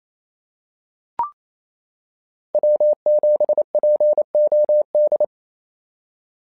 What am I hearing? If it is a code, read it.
Morse code, heard as a keyed tone, about 27 wpm, 590 Hz. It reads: W7POD
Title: SOTA QSO Practice 28wpm